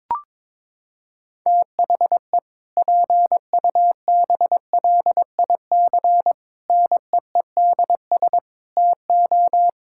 Morse code, heard as a keyed tone, about 22 wpm, 700 Hz. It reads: THE PUBLIC NEEDS TO
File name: practice / 22wpm / Sentences with 2-Word Phrases - 22wpm